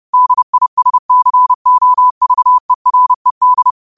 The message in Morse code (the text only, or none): DISCOVERED